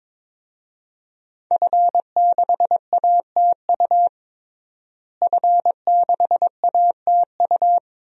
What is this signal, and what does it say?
Morse code, heard as a keyed tone, about 22 wpm, 700 Hz. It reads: F6ATU F6ATU
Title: DXCC-100 Call Signs - 5 Characters - Repeated 3x - 22wpm